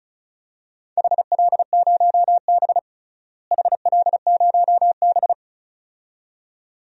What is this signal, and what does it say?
Morse code, heard as a keyed tone, about 36 wpm, 700 Hz. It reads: HL0B HL0B
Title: DXCC-100 Call Signs - 4 Characters - Repeated 2x - 35wpm